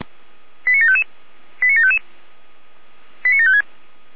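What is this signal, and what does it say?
Five-tone selective call (ZVEI), heard as quick runs of five tones, tones of about 70 ms. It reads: 89755 89755 89756
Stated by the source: Five-tone ZVEI selective call 89531 received on an SDR.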